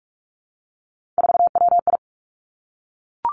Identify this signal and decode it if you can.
Morse code, heard as a keyed tone, about 46 wpm, 700 Hz. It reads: 4WI
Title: Three Challenging Letter-Number Combinations 45wpm